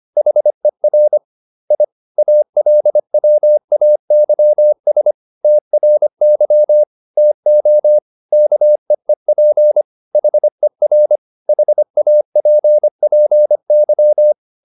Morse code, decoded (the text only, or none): HER I ALWAYS TRY TO KEEP HER HAPPY